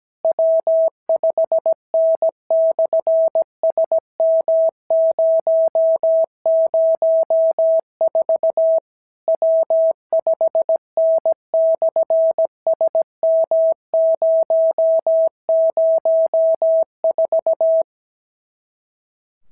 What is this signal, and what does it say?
Morse code, heard as a keyed tone, about 17 wpm, 650 Hz. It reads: W5N/SM004 W5N/SM004
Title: SOTA QSO Practice 17wpm